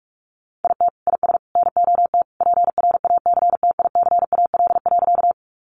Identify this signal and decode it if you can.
Morse code, heard as a keyed tone, about 45 wpm, 700 Hz. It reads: IT IS NOT PRACTICAL.
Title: Sentences from Top 1400 Words 45wpm